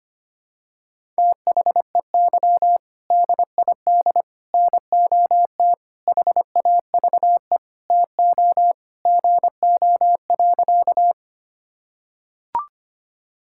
Morse code, heard as a keyed tone, about 25 wpm, 700 Hz. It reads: THEY DID NOT HAVE TO GO.
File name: practice / 25wpm / Sentences with 4-Word Phrases - 25wpm